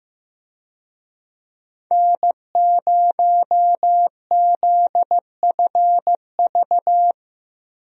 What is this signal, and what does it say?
Morse code, heard as a keyed tone, about 15 wpm, 700 Hz. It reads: N0ZFV
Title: USA Call Signs - 5 Characters - Repeated 2x - 15wpm